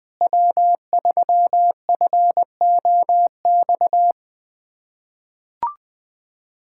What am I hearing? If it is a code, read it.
Morse code, heard as a keyed tone, about 20 wpm, 700 Hz. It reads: W3FOX
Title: USA Callsign DE Callsign - 5 Characters - 20wpm